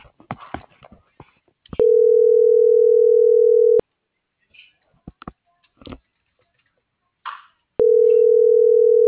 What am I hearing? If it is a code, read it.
Ringback tone (North America), 440 + 480 Hz.